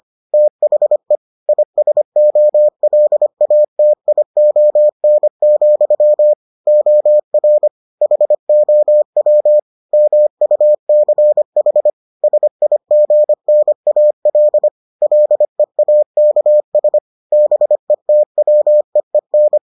Morse code, heard as a keyed tone, about 25 wpm, 600 Hz. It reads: THE ISOLATION, OR HOW MUCH SIGNAL LEAKS BETWEEN